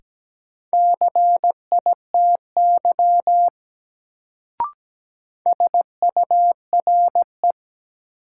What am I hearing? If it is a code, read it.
Morse code, heard as a keyed tone, about 17 wpm, 700 Hz. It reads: CITY SURE
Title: Sentences from Top 300 Words - New Word Drill - 17wpm